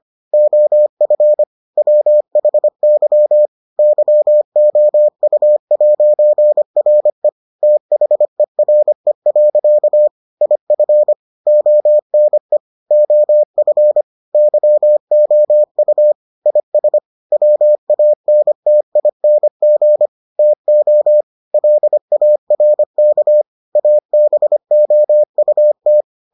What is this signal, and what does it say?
Morse code, heard as a keyed tone, about 25 wpm, 600 Hz. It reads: OF WHY YOU'RE THERE. IF ONE OF YOU IS WANTING TO LARK ABOUT